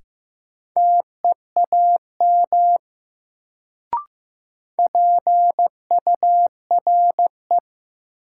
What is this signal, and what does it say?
Morse code, heard as a keyed tone, about 15 wpm, 700 Hz. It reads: TEAM PURE
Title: Sentences from Top 1500 Words - New Word Drill - 15wpm